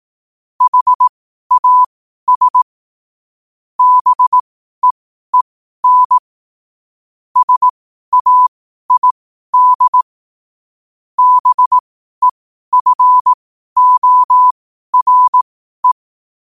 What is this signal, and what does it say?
Morse code, heard as a keyed tone, about 18 wpm, spaced as for 12 wpm, 1000 Hz. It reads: HAS BEEN SAID BEFORE
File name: qotdfiles / 12wpm